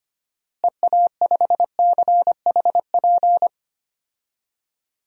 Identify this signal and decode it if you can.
Morse code, heard as a keyed tone, about 25 wpm, 700 Hz. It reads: EA5CHP
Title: DXCC-100 Call Signs - 6 Characters - Repeated 2x - 25wpm